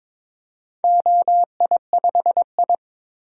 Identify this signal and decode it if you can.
Morse code, heard as a keyed tone, about 22 wpm, 700 Hz. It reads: OI5I
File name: practice / 22wpm / Four Letter-Number Combinations - 22wpm